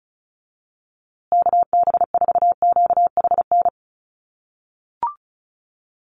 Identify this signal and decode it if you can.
Morse code, heard as a keyed tone, about 34 wpm, 700 Hz. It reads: KB4QHN
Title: USA Call Signs Repeated 3x 35wpm